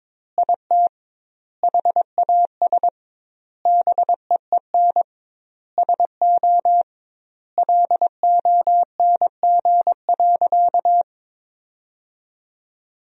Morse code, heard as a keyed tone, about 22 wpm, 700 Hz. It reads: IT HAS BEEN SO LONG.
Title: 2x Word-Spacing - Sentences from Top 100 Words - 22wpm